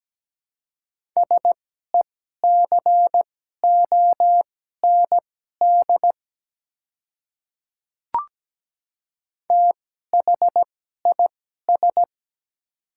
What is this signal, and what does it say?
Morse code, heard as a keyed tone, about 17 wpm, spaced as for 12 wpm, 700 Hz. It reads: SECOND THIS
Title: 2x Character Spacing - Top 300 Words - 17wpm